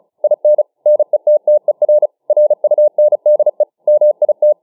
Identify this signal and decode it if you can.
Morse code, heard as a keyed tone, about 34 wpm, 600 Hz. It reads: IN NETTER RUNDE MIT